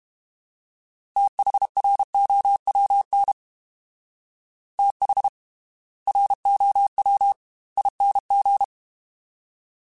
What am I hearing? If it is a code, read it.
Morse code, heard as a keyed tone, about 32 wpm, spaced as for 25 wpm, 800 Hz. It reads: THROWN THROWING
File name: TH / TH mp3s / Fast TH @30